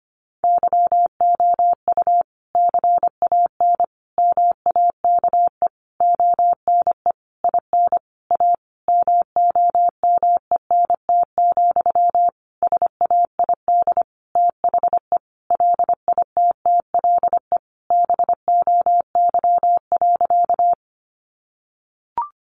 Morse code, heard as a keyed tone, about 25 wpm, 700 Hz. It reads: YOU CAN MAKE ONE IN A MOMENT, SAID THE LITTLE BOY.
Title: Sentences from Top 1000 Words 25wpm